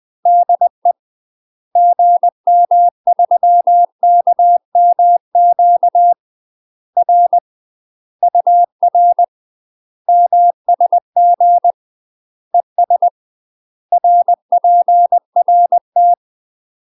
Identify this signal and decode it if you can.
Morse code, heard as a keyed tone, about 20 wpm, 700 Hz. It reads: DE GM3KMQ R UR MSG ES RPRT
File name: UK MASTER QSOs TRAINING / 20WPM / UK MASTER QSOs 0015